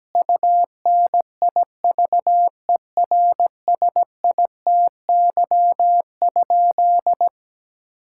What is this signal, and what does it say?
Morse code, heard as a keyed tone, about 17 wpm, 700 Hz. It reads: UNIVERSITY?
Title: Sentences from Top 1000 Words - Review Part 5 - 17wpm